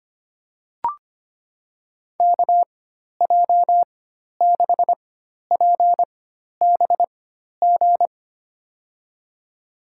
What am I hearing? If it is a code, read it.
Morse code, heard as a keyed tone, about 25 wpm, spaced as for 12 wpm, 700 Hz. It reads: KJ6PBG
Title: USA Call Signs - 6 Characters - 4x Character Spacing - 25wpm